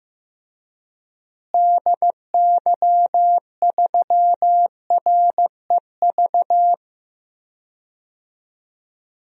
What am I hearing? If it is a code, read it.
Morse code, heard as a keyed tone, about 15 wpm, 700 Hz. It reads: DY3REV
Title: DXCC-100 Call Signs - 6 Characters - Repeated 3x - 15wpm